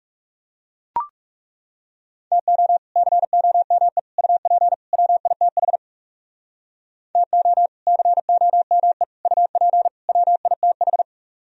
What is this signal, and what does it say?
Morse code, heard as a keyed tone, about 40 wpm, 700 Hz. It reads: TO COME UP WITH TO COME UP WITH